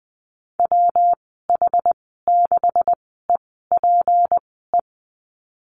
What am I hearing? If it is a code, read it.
Morse code, heard as a keyed tone, about 19 wpm, spaced as for 14 wpm, 700 Hz. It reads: WH6EPE
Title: 173 Lesson 28g - Number 9 - Call Signs 20wpm